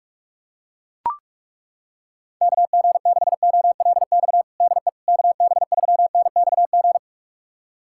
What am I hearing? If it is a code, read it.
Morse code, heard as a keyed tone, about 41 wpm, 700 Hz. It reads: KG6OLX DE KB3NXG
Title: USA Callsign DE Callsign - 6 Characters - 45wpm